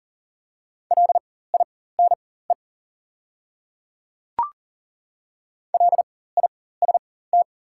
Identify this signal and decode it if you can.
Morse code, heard as a keyed tone, about 40 wpm, spaced as for 19 wpm, 700 Hz. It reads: LINE LIST
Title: 4x Character Spacing - Top 500 Words - 40wpm